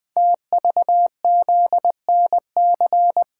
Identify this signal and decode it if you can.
Morse code, heard as a keyed tone, about 20 wpm, 700 Hz. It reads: TVZNC
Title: Code Groups - 5 Letter:Numbers - 20wpm